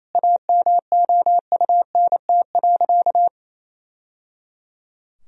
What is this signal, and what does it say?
Morse code, heard as a keyed tone, about 28 wpm, 700 Hz. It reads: AMOUNT.
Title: Sentences from Top 1000 Words - Review Part 4 - 28wpm